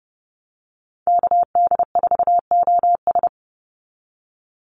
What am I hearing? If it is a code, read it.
Morse code, heard as a keyed tone, about 30 wpm, 700 Hz. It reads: KD4OS
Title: Call Signs Repeated 3x - 30wpm